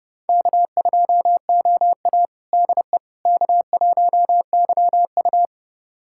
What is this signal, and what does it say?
Morse code, heard as a keyed tone, about 30 wpm, 700 Hz. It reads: K2OA DE K1YU
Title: USA Callsign DE Callsign - 4 Characters - 30wpm